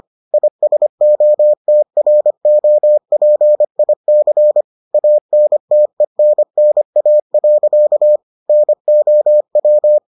Morse code, heard as a keyed tone, about 25 wpm, 600 Hz. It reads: ISOTROPIC ANTENNA. NOW